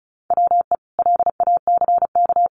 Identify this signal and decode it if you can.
Morse code, heard as a keyed tone, about 34 wpm, 700 Hz. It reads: WE LACK
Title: Sentences from Top 1400 Words 35wpm